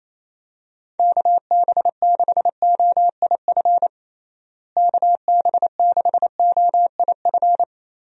Morse code, heard as a keyed tone, about 28 wpm, 700 Hz. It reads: KB6OIF KB6OIF
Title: USA Call Signs - 6 Characters - Repeated 2x - 28wpm